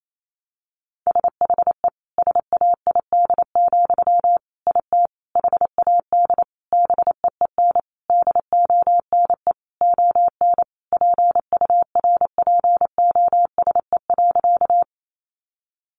Morse code, heard as a keyed tone, about 28 wpm, 700 Hz. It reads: SHE SAID, IT HAD BEEN DONE ON PURPOSE.
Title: Sentences from Top 800 Words 28wpm